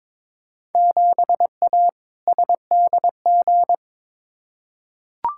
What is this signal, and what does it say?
Morse code, heard as a keyed tone, about 22 wpm, 700 Hz. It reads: 7A SDG
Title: Field Day Exchanges Encore - 22wpm